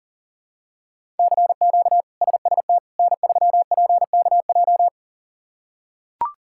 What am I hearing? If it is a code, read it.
Morse code, heard as a keyed tone, about 40 wpm, 700 Hz. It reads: CQ SST N3PKJ
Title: K1USN SST QSO Practice - 40wpm